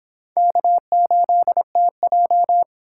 Morse code, heard as a keyed tone, about 30 wpm, 700 Hz. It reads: K8TJ